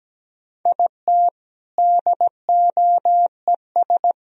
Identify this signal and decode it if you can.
Morse code, heard as a keyed tone, about 17 wpm, 700 Hz. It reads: IT DOES
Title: Speed-Racing - Sets of 4 Words 17wpm